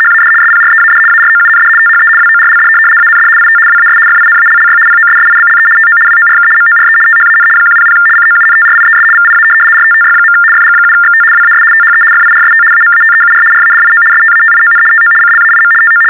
RTTY, about 75 baud. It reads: HAD SPREAD AROUND THE WORLD AND IT WAS FINALLY POSSIBLE TO WORK MORE THAN 100 COUNTRIES VIA RTTY. FG7XT WAS THE FIRST AMATEUR RADIO STATION TO CLAIM TO
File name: teletype-RTTY-75